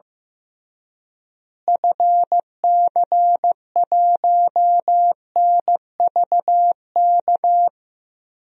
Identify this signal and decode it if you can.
Morse code, heard as a keyed tone, about 15 wpm, 700 Hz. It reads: FC1NVK
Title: DXCC-100 Call Signs - 6 Characters - Repeated 3x - 15wpm